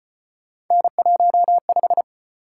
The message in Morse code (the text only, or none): N15